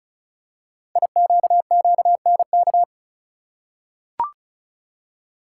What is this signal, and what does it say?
Morse code, heard as a keyed tone, about 34 wpm, 690 Hz. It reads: IQQNK
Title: Code Groups - 5 Letters - 35wpm